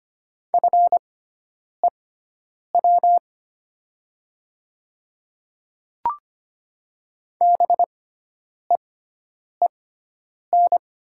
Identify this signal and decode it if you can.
Morse code, heard as a keyed tone, about 25 wpm, spaced as for 9 wpm, 700 Hz. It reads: FEW BEEN
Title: 6x Character Spacing - Top 300 Words - 25wpm